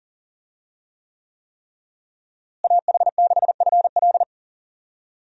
Morse code, heard as a keyed tone, about 39 wpm, 700 Hz. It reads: AH6FL
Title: USA Call Signs - 5 Characters - Repeated 3x - 40wpm